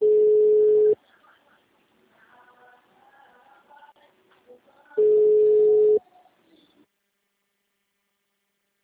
Ringback tone (Europe, 425 Hz).
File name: out- -2010- - - .69208